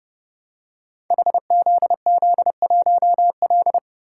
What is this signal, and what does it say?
Morse code, heard as a keyed tone, about 30 wpm, 700 Hz. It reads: HZZ1L
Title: Code Groups - 5 Letter:Numbers - 30wpm